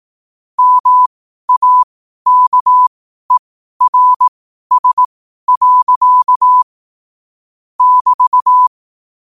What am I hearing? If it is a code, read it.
Morse code, heard as a keyed tone, about 18 wpm, spaced as for 12 wpm, 1000 Hz. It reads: MAKERS. <BT>
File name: qotdfiles / 12wpm